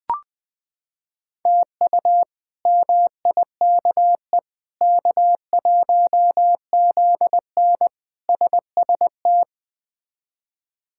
Morse code, heard as a keyed tone, about 20 wpm, 700 Hz. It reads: TU MIKE K1ZN SST
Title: K1USN SST QSO Practice - 20wpm